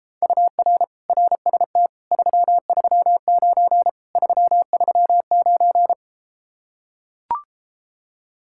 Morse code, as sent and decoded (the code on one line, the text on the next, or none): ..- .-. / .-. ... - / ...-- ...-- ----. / ...-- ...-- ----.
UR RST 339 339